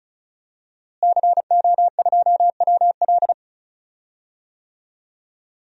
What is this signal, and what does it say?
Morse code, heard as a keyed tone, about 36 wpm, 700 Hz. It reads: CO2WL
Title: DXCC-100 Call Signs - 5 Characters - Repeated 3x - 35wpm